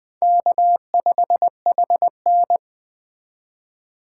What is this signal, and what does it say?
Morse code, heard as a keyed tone, about 20 wpm, 700 Hz. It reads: K5HN